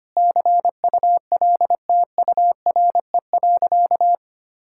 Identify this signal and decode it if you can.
Morse code, heard as a keyed tone, about 25 wpm, 700 Hz. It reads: CULTURE.
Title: Sentences from Top 1000 Words - Review Part 6 - 25wpm